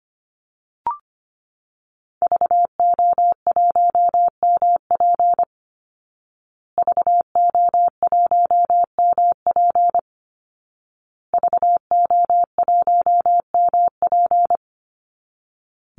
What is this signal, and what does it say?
Morse code, heard as a keyed tone, about 25 wpm, 700 Hz. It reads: VO1MP VO1MP VO1MP
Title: Call Signs Repeated 2x - 25wpm